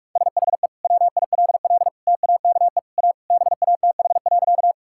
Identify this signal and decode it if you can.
Morse code, heard as a keyed tone, about 46 wpm, 700 Hz. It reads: SHE WILL TAKE A BATH.
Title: Sentences from Top 1600 Words 45wpm